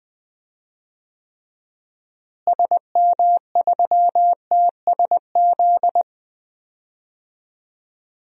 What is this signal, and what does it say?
Morse code, heard as a keyed tone, about 20 wpm, 690 Hz. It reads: SM3TSZ
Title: DXCC-100 Call Signs - 6 Characters - Repeated 2x - 20wpm